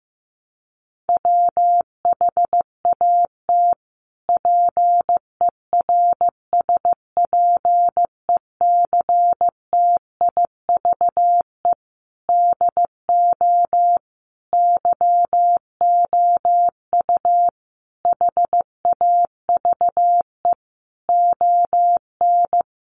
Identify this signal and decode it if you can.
Morse code, heard as a keyed tone, about 15 wpm, 700 Hz. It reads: WHAT PERSPECTIVE DO YOU HAVE ON
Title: Sentences from Top 1200 Words 15wpm